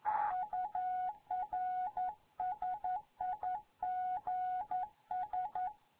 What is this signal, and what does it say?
Morse code, heard as a keyed tone, about 11 wpm, 740 Hz. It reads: UR SIGS
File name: Morze-HA2MN